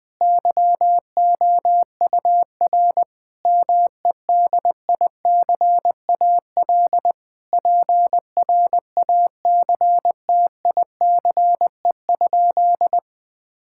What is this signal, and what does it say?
Morse code, heard as a keyed tone, about 20 wpm, 700 Hz. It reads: YOUR MEDICAL PRACTICE?
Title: Sentences from Top 1000 Words - Review Part 5 - 20wpm